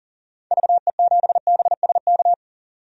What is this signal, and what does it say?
Morse code, heard as a keyed tone, about 40 wpm, 700 Hz. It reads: VE7BSK
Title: DXCC-100 Call Signs - 6 Characters - 40wpm